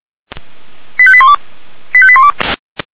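Five-tone selective call (ZVEI), heard as quick runs of five tones, tones of about 70 ms. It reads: 85812 85812
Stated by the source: ZVEI five-tone call 89347 received by SDR.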